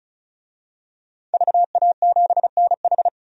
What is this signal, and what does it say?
Morse code, heard as a keyed tone, about 34 wpm, 700 Hz. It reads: VA7NH
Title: DXCC-100 Call Signs - 5 Characters - 35wpm